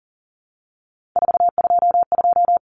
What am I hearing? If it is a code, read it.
Morse code, heard as a keyed tone, about 40 wpm, 700 Hz. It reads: 422
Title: Common Cut Numbers - 3-Digit 40wpm